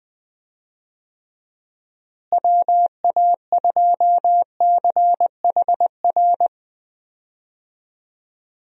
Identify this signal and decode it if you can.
Morse code, heard as a keyed tone, about 20 wpm, 700 Hz. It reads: WA2CHR
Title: USA Call Signs - 6 Characters - Repeated 2x - 20wpm